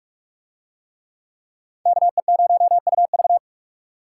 Morse code, heard as a keyed tone, about 45 wpm, 700 Hz. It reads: KE0UV